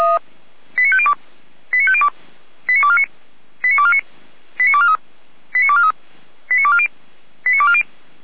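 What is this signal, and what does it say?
Five-tone selective call (ZVEI), heard as quick runs of five tones, tones of about 70 ms. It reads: 89502 89502 89249 89249 89243 89243 89240 89240
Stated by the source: ZVEI five-tone call 89250 received by SDR, channel 455, Feuerwehr Orlamuende.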